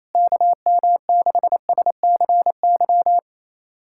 Morse code, heard as a keyed tone, about 27 wpm, 700 Hz. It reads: KM6SCY
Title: USA Callsign DE Callsign 28wpm